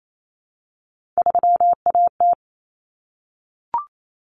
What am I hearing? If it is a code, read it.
Morse code, heard as a keyed tone, about 28 wpm, 700 Hz. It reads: 3AT
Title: Common Cut Numbers - 3-Digit 28wpm